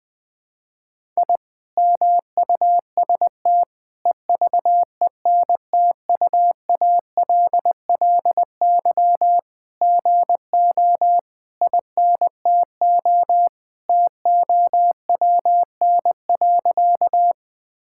Morse code, heard as a keyed tone, about 20 wpm, 700 Hz. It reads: I MUST EVENTUALLY GO INTO TOWN.
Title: Sentences from Top 1000 Words - Review Part 6 - 20wpm